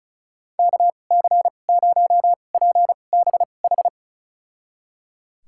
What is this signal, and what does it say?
Morse code, heard as a keyed tone, about 36 wpm, spaced as for 25 wpm, 700 Hz. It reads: KC0PBH
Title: USA Call Signs - 6 Characters - 2x Character Spacing - 35wpm